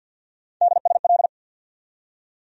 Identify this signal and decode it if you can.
Morse code, heard as a keyed tone, about 47 wpm, 700 Hz. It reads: DSL
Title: USA Call Sign Postfixes - 3 Letters 50wpm